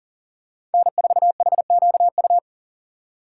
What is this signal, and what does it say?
Morse code, heard as a keyed tone, about 40 wpm, 700 Hz. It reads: N4HQU
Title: USA Call Signs - 2x Character Spacing - 40wpm